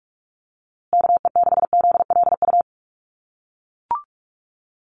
Morse code, heard as a keyed tone, about 44 wpm, 700 Hz. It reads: KE6ZLU
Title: USA Call Signs Repeated 2x 45wpm